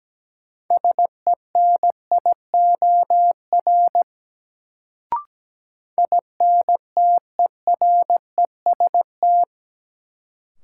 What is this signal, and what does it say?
Morse code, heard as a keyed tone, about 17 wpm, 700 Hz. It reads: SENIOR INTEREST
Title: Sets of 5 Words Encore - New Word Drill - 17wpm